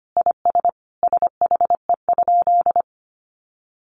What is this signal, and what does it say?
Morse code, heard as a keyed tone, about 25 wpm, 700 Hz. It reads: IS SHE?
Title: Sentences from Top 100 Words 25wpm